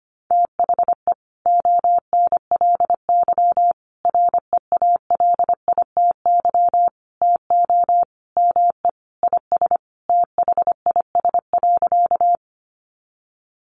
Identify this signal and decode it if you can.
Morse code, heard as a keyed tone, about 25 wpm, 700 Hz. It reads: THE ONLY REALITY TO ME IS THIS.
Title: Sentences from Top 1200 Words 25wpm